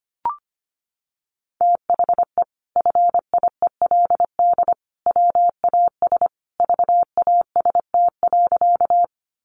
Morse code, heard as a keyed tone, about 25 wpm, 700 Hz. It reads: THE FIELD WAS VAST.
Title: Sentences from Top 1300 Words 25wpm